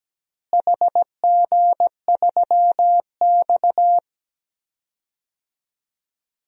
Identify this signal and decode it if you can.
Morse code, heard as a keyed tone, about 17 wpm, 700 Hz. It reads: HG3X
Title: DXCC-100 Call Signs - 4 Characters - Repeated 2x - 17wpm